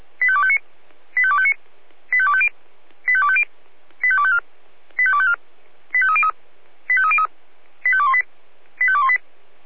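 Five-tone selective call (ZVEI), heard as quick runs of five tones, tones of about 70 ms. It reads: 85259 85259 85250 85250 85254 85254 85303 85303 85218 85218
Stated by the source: ZVEI five-tone selective call received via SDR.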